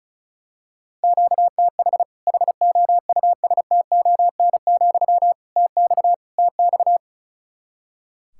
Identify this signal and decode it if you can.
Morse code, heard as a keyed tone, about 34 wpm, 700 Hz. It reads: QTH HOUSTON, TX TX
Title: Practice QSO - Exchange 1 - Beginning - 35wpm